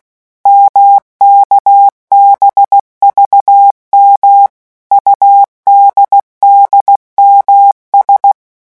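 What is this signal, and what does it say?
Morse code, heard as a keyed tone, about 16 wpm, 800 Hz. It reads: MKBVM UDDMS
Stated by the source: Morse letter-group revision drill at 15 words per minute.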